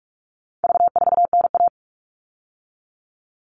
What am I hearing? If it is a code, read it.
Morse code, heard as a keyed tone, about 44 wpm, 700 Hz. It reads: V4NA
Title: All Cut Numbers - 4-Digit 45wpm